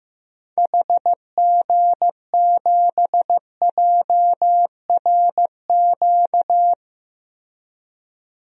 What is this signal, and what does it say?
Morse code, heard as a keyed tone, about 15 wpm, 700 Hz. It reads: HG7JRQ
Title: DXCC-100 Call Signs - 6 Characters - Repeated 2x - 15wpm